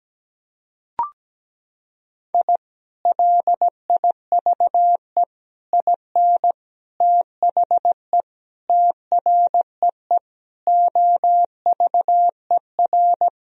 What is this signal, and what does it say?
Morse code, heard as a keyed tone, about 17 wpm, 700 Hz. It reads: I LIVE IN THE TREE OVER
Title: Sentences from Top 1000 Words - Review Part 1 - 17wpm